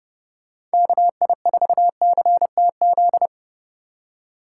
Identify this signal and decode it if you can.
Morse code, heard as a keyed tone, about 30 wpm, 700 Hz. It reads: KI4CTZ